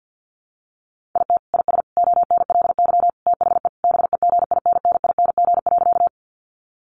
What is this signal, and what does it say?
Morse code, heard as a keyed tone, about 50 wpm, 700 Hz. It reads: IT IS ONLY THE BEGINNING.
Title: Sentences from Top 800 Words 50wpm